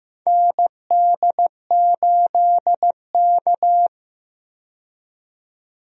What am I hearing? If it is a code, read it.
Morse code, heard as a keyed tone, about 15 wpm, 700 Hz. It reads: ND8K
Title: USA Call Signs - 4 Characters - 15wpm